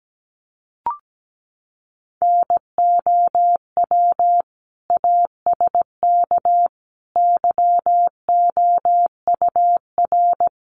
Morse code, heard as a keyed tone, about 17 wpm, 700 Hz. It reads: NOW ASK YOUR
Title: Sentences from Top 1100 Words 17wpm